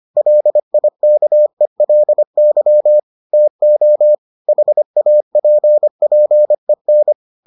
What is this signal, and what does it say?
Morse code, heard as a keyed tone, about 25 wpm, 600 Hz. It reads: LIKELY TO HAPPEN